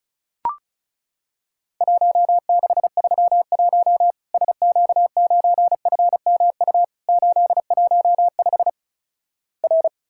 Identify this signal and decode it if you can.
Morse code, heard as a keyed tone, about 36 wpm, 700 Hz. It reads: 1631 SQ9FMU 815 R
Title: Worked All Europe DX Contest - QTC Exchange - 35wpm